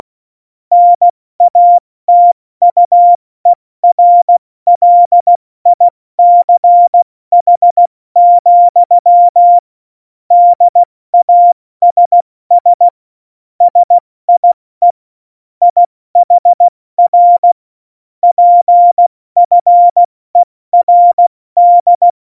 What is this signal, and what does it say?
Morse code, heard as a keyed tone, about 16 wpm, 700 Hz. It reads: NATUERLICH, DASS SIE IHR PFERD
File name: Morsekurs / Witze / Witz 05 Mai, 2020 70 F80